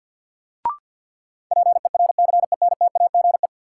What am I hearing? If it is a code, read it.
Morse code, heard as a keyed tone, about 52 wpm, 700 Hz. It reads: PERCENTAGE
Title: Sentences from Top 1300 Words - New Word Drill - 50wpm